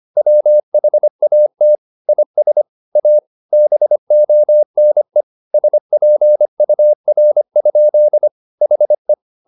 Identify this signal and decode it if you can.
Morse code, heard as a keyed tone, about 25 wpm, 600 Hz. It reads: WHAT IS A BONE SPUR? HE
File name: the old man and the sea part 1 25wpm